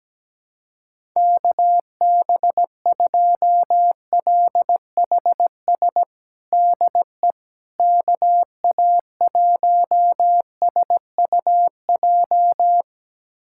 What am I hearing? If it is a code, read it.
Morse code, heard as a keyed tone, about 17 wpm, 700 Hz. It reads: KB2LHS DE KA1SUJ
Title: USA Callsign DE Callsign 17wpm